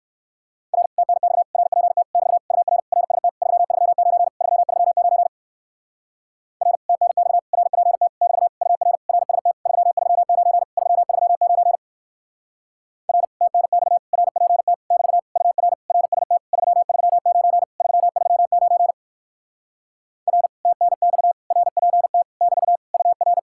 Morse code, heard as a keyed tone, about 59 wpm, 700 Hz. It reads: R TNX RPT <BT> UR RST 339 339 R TNX RPT <BT> UR RST 339 339 R TNX RPT <BT> UR RST 339 339 R TNX RPT <BT> UR